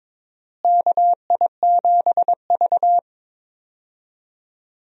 Morse code, heard as a keyed tone, about 22 wpm, 700 Hz. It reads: KI7V
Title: USA Call Signs - 4 Characters - Repeated 2x - 22wpm